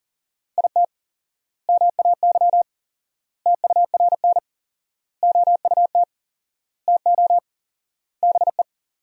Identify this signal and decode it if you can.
Morse code, heard as a keyed tone, about 40 wpm, 700 Hz. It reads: IT MAY TURN OUT TO BE
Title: 4x Word-Spacing - Sentences from Top 400 Words - 40wpm